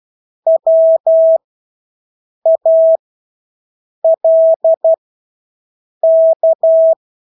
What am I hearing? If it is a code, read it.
Morse code, heard as a keyed tone, about 12 wpm, spaced as for 6 wpm, 650 Hz. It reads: WALK
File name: Engelske ord 12-6 wpm